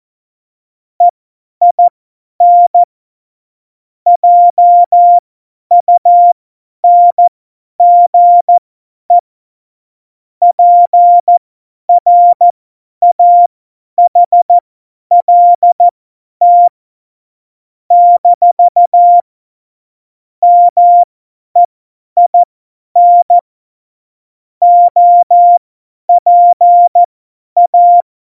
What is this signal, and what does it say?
Morse code, heard as a keyed tone, about 14 wpm, spaced as for 10 wpm, 700 Hz. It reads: EIN JUNGE PRAHLT - MEIN OPA
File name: Morsekurs / Witze / Witz 27. Oktober 2020, 50 F70